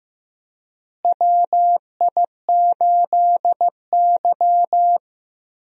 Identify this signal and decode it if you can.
Morse code, heard as a keyed tone, about 15 wpm, 700 Hz. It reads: WI8Y
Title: USA Call Signs - 4 Characters - Repeated 3x - 15wpm